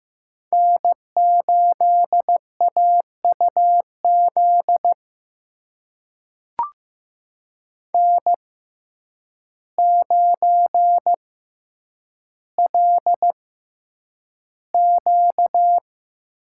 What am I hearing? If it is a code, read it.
Morse code, heard as a keyed tone, about 15 wpm, spaced as for 8 wpm, 700 Hz. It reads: N8AUZ N9LQ
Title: USA Call Signs - 5 Characters - 6x Character Spacing - 15wpm